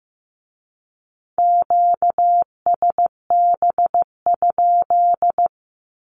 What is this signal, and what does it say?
Morse code, heard as a keyed tone, about 15 wpm, 700 Hz. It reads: QSB?
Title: Q-Codes with Spoken Meaning - 15wpm